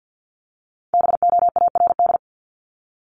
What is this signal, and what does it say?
Morse code, heard as a keyed tone, about 50 wpm, 700 Hz. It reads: BOARD
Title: Top 501-750 Words 50wpm